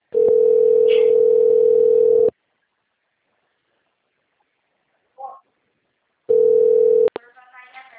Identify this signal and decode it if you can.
Ringback tone (North America), 440 + 480 Hz.
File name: rg-1600-unknown- - - .5606